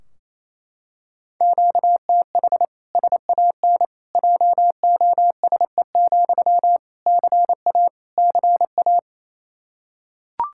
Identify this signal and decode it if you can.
Morse code, heard as a keyed tone, about 28 wpm, 700 Hz. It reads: QTH SAN JOSE, CA CA
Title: Practice QSO - Exchange 1 - Beginning - 28wpm